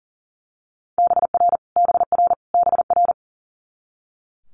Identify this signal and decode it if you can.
Morse code, heard as a keyed tone, about 40 wpm, 700 Hz. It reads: BR BR BR
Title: Multi-letter Phonemes Repeated 3x 40wpm